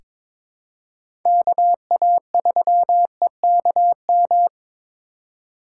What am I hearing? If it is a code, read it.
Morse code, heard as a keyed tone, about 22 wpm, 700 Hz. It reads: KA3EKM
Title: USA Call Signs - 6 Characters - 3x Character Spacing - 22wpm